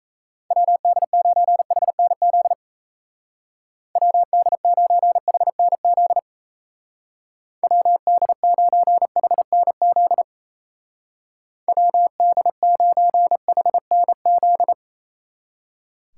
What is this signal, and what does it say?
Morse code, heard as a keyed tone, about 37 wpm, 700 Hz. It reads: WD9HNZ WD9HNZ WD9HNZ WD9HNZ